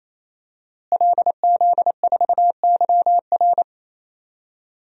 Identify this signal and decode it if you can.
Morse code, heard as a keyed tone, about 28 wpm, 700 Hz. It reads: LZ4YR
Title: DXCC-100 Call Signs - 5 Characters - Repeated 3x - 28wpm